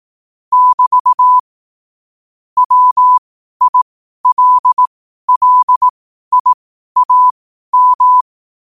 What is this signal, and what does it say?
Morse code, heard as a keyed tone, about 18 wpm, spaced as for 12 wpm, 1000 Hz. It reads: <BT> WILLIAM
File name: qotdfiles / 12wpm